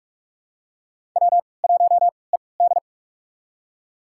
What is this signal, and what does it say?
Morse code, heard as a keyed tone, about 45 wpm, spaced as for 25 wpm, 700 Hz. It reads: W1ED
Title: USA Call Signs - 4 Characters - 3x Character Spacing - 45wpm